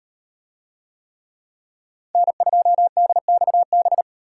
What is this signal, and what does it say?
Morse code, heard as a keyed tone, about 38 wpm, 700 Hz. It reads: N2DXB